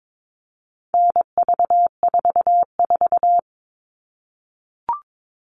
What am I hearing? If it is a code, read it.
Morse code, heard as a keyed tone, about 22 wpm, 700 Hz. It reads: NV44
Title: All Cut Numbers - 4-Digit 22wpm